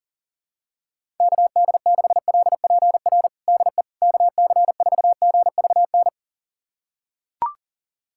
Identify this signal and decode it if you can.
Morse code, heard as a keyed tone, about 37 wpm, 700 Hz. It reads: KD6LPR DE KC4GVN
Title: USA Callsign DE Callsign - 6 Characters - 40wpm